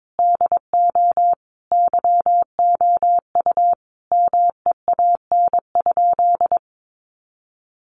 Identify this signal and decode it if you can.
Morse code, heard as a keyed tone, about 22 wpm, 700 Hz. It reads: DO YOU MEAN?
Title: Sentences from Top 500 Words 22wpm